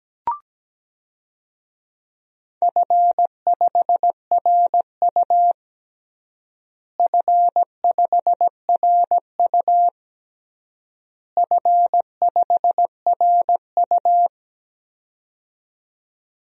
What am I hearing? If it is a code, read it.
Morse code, heard as a keyed tone, about 17 wpm, 700 Hz. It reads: F5RU F5RU F5RU
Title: DXCC-100 Call Signs - 4 Characters - Repeated 3x - 17wpm